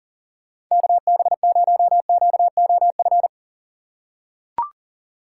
Morse code, heard as a keyed tone, about 40 wpm, 700 Hz. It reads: KB0QOF
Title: USA Call Signs - 6 Characters - 3x Character Spacing - 40wpm